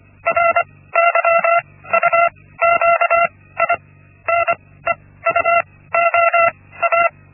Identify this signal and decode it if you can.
Morse code, heard as a keyed tone, about 25 wpm, spaced as for 16 wpm, 680 Hz. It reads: RYUQI NEUOA
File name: Linksmakalnis-1